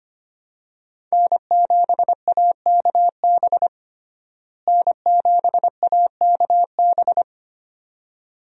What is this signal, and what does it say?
Morse code, heard as a keyed tone, about 25 wpm, 700 Hz. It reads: N7AKB N7AKB
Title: USA Call Signs - 5 Characters - Repeated 2x - 25wpm